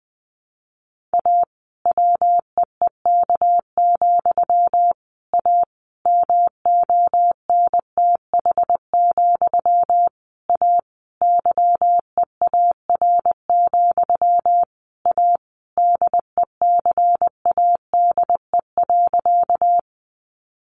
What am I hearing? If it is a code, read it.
Morse code, heard as a keyed tone, about 20 wpm, 700 Hz. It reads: A WEEK, A MONTH, A YEAR, A DECADE.
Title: A Sound Of Thunder 20wpm